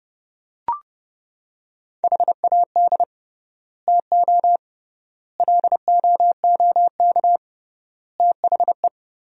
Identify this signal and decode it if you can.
Morse code, heard as a keyed tone, about 30 wpm, 700 Hz. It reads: HAD TO LOOK THE
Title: 3x Word-Spacing - Sentences from Top 100 Words - 30wpm